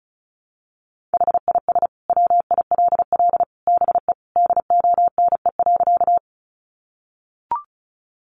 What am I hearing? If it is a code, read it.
Morse code, heard as a keyed tone, about 34 wpm, 700 Hz. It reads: HIS WILL BE DONE.
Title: Sentences from Top 500 Words 35wpm